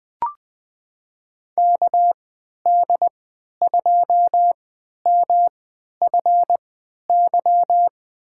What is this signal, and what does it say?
Morse code, heard as a keyed tone, about 21 wpm, spaced as for 11 wpm, 700 Hz. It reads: KD2MFY
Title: USA Call Signs - 6 Characters - 3x Character Spacing - 20wpm